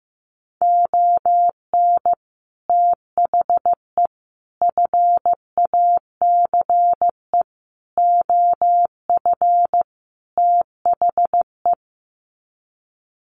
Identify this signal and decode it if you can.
Morse code, heard as a keyed tone, about 15 wpm, 700 Hz. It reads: ON THE FACE OF THE
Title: Sets of 5 Words Encore2 15wpm